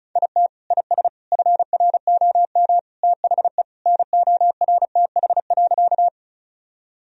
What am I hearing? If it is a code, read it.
Morse code, heard as a keyed tone, about 34 wpm, 700 Hz. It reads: IT IS FROM THE NORTH.
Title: Sentences from Top 1000 Words - Review Part 1 - 35wpm